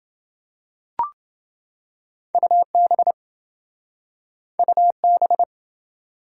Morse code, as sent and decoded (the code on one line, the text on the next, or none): ..- -... / ..- -...
UB UB